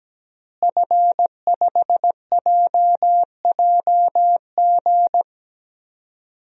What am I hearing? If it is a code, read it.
Morse code, heard as a keyed tone, about 17 wpm, 700 Hz. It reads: F5JJG
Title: DXCC-100 Call Signs - 5 Characters - Repeated 2x - 17wpm